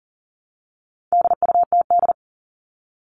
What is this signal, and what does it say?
Morse code, heard as a keyed tone, about 40 wpm, 700 Hz. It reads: DUTD
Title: All Cut Numbers - 4-Digit 40wpm